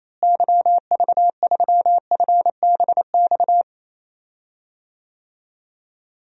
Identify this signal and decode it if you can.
Morse code, heard as a keyed tone, about 28 wpm, 700 Hz. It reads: YV3FBX
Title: DXCC-100 Call Signs - 6 Characters - Repeated 3x - 28wpm